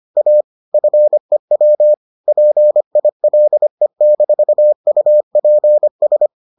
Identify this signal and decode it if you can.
Morse code, heard as a keyed tone, about 25 wpm, 600 Hz. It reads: A FEW PILE-UPS